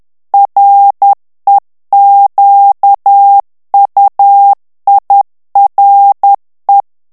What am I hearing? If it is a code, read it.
Morse code, heard as a keyed tone, about 10 wpm, 800 Hz. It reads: REQUIRE
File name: ANGLAIS texte clair 3 vitesse 10 mots minute